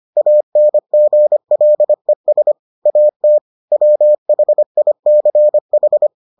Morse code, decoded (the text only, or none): ANGLES AT WHICH